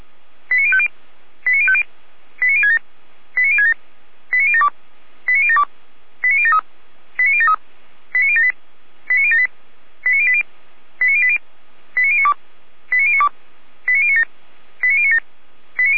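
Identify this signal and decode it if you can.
Five-tone selective call (ZVEI), heard as quick runs of five tones, tones of about 70 ms. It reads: 89050 89050 89067 89067 89072 89072 89073 89073 89078 89078 89080 89080 89092 89092 89097 89097
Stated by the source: ZVEI five-tone call 89022 received by SDR, from fire brigade Feuerwehr Orlamünde's audio archive (f85360).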